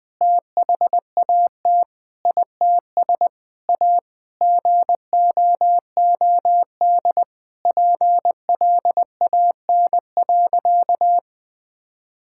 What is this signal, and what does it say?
Morse code, heard as a keyed tone, about 20 wpm, 700 Hz. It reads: THAT ITS A GOOD PLAN.
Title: Sentences with 5-Word Phrases - 20wpm